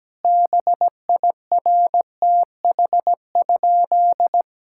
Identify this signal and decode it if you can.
Morse code, heard as a keyed tone, about 17 wpm, 700 Hz. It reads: BIRTH?
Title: Sentences from Top 1000 Words - Review Part 5 - 17wpm